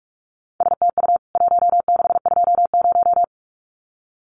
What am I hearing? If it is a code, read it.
Morse code, heard as a keyed tone, about 44 wpm, 700 Hz. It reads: STU 1620
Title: CWT Exchange 45wpm